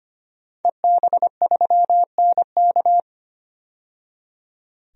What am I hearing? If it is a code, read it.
Morse code, heard as a keyed tone, about 25 wpm, 690 Hz. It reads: EB3NK
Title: DXCC-100 Call Signs - 5 Characters - 25wpm